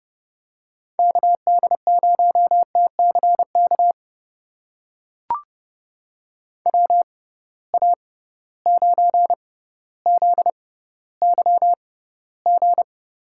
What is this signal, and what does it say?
Morse code, heard as a keyed tone, about 30 wpm, spaced as for 15 wpm, 700 Hz. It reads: KD0TCK WA9ZYG
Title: USA Call Signs - 6 Characters - 6x Character Spacing - 30wpm